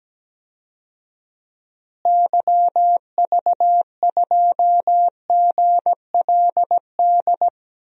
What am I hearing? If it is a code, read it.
Morse code, heard as a keyed tone, about 17 wpm, 700 Hz. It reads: YV2GLD
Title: DXCC-100 Call Signs - 6 Characters - Repeated 2x - 17wpm